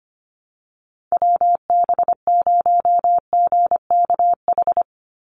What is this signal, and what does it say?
Morse code, heard as a keyed tone, about 25 wpm, 700 Hz. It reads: WB0GKH
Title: Call Signs Repeated 3x - 25wpm